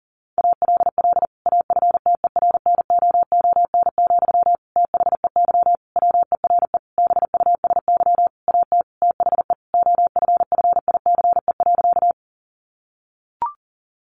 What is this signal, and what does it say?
Morse code, heard as a keyed tone, about 40 wpm, 700 Hz. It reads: ALL AFTERNOON, THEY WERE BUSY AT THE OFFICE.
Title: Sentences from Top 1500 Words 40wpm